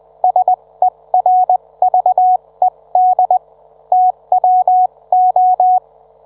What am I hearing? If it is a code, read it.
Morse code, heard as a keyed tone, about 20 wpm, spaced as for 17 wpm, 740 Hz. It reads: SERVED TWO